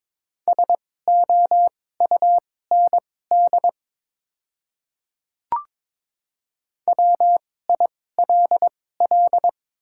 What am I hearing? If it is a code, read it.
Morse code, heard as a keyed tone, about 23 wpm, spaced as for 16 wpm, 700 Hz. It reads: SOUND WILL
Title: 2x Character Spacing - Top 100 Words - 22wpm